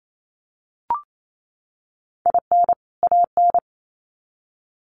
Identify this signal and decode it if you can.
Morse code, heard as a keyed tone, about 28 wpm, 700 Hz. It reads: IN AN
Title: Sets of 2 Words 28wpm